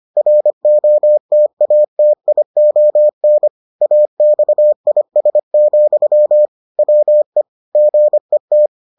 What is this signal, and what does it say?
Morse code, heard as a keyed tone, about 25 wpm, 600 Hz. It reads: ROTATION AXIS, WE GET